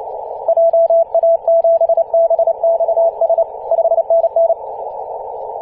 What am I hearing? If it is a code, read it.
Morse code, heard as a keyed tone, about 30 wpm, 670 Hz. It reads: JA7BXS 5NN
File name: ep2a-21-029mhz-cw